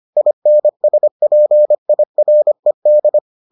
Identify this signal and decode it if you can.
Morse code, heard as a keyed tone, about 25 wpm, 600 Hz. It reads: INSPIRED